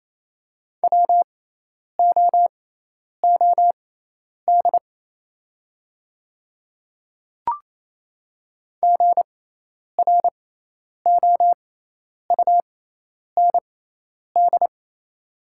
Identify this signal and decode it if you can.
Morse code, heard as a keyed tone, about 28 wpm, spaced as for 10 wpm, 700 Hz. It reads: WOOD GROUND
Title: 6x Character Spacing - Top 500 Words - 28wpm